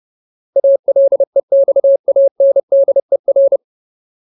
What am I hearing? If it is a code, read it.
Morse code, heard as a keyed tone, about 30 wpm, 550 Hz. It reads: ALEXANDER
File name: Ultimate Word List / 30WPM Ultimate